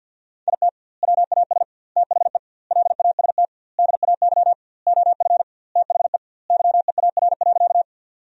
Morse code, heard as a keyed tone, about 50 wpm, 700 Hz. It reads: IT WAS THE LAST DAY OF THE YEAR.
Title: 2x Word-Spacing - Sentences from Top 300 Words - 50wpm